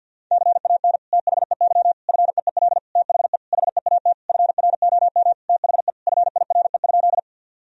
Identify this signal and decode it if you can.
Morse code, heard as a keyed tone, about 50 wpm, 700 Hz. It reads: CAN THEY FEEL THE HEAT FROM THE FIRE?
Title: Sentences from Top 1000 Words - Review Part 3 - 50wpm